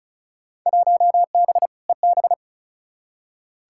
Morse code, heard as a keyed tone, about 36 wpm, 700 Hz. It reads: 1B EB
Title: Field Day Exchanges - 35wpm